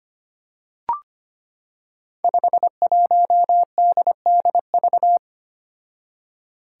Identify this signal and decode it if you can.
Morse code, heard as a keyed tone, about 25 wpm, 700 Hz. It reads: 51DDV
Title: Five Challenging Letter-Number Combinations 25wpm